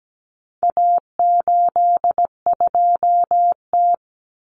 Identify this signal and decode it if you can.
Morse code, heard as a keyed tone, about 17 wpm, 700 Hz. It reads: A82T
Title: Common Cut Numbers - 4-Digit 17wpm